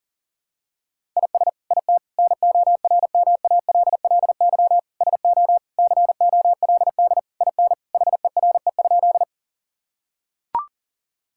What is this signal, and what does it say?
Morse code, heard as a keyed tone, about 40 wpm, 700 Hz. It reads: IS IT NORMALLY SO COLD IN HERE?
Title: Sentences from Top 1000 Words - Review Part 6 - 40wpm